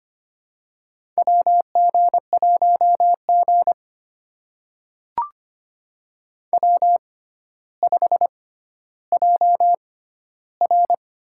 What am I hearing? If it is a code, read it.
Morse code, heard as a keyed tone, about 25 wpm, spaced as for 13 wpm, 700 Hz. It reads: WG1G W5JR
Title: USA Call Signs - 4 Characters - 6x Character Spacing - 25wpm